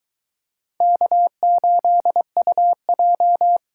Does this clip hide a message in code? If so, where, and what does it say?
Yes: Morse code from 0.8 to 3.6 s, heard as a keyed tone, about 22 wpm, 700 Hz. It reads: K8UJ